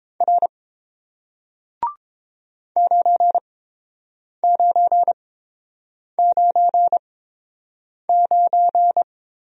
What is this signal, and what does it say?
Morse code, heard as a keyed tone, about 30 wpm, spaced as for 7 wpm, 700 Hz. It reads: R 9999